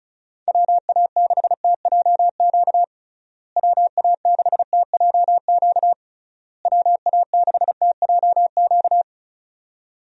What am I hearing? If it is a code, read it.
Morse code, heard as a keyed tone, about 36 wpm, 700 Hz. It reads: WA6TJQ WA6TJQ WA6TJQ
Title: USA Call Signs - 6 Characters - Repeated 3x - 35wpm